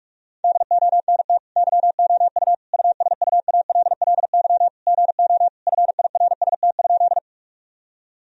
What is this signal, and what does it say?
Morse code, heard as a keyed tone, about 46 wpm, 700 Hz. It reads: DONT YOU USUALLY GO FIRST?
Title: Sentences from Top 1000 Words - Review Part 3 - 45wpm